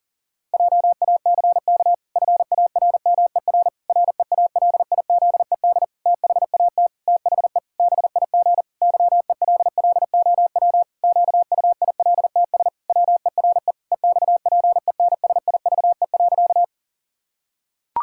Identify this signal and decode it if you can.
Morse code, heard as a keyed tone, about 40 wpm, 700 Hz. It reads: JACK FARMER REALIZED THAT THE BIG YELLOW QUILTS WERE EXPENSIVE.
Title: Pangram Challenge v2 - 40wpm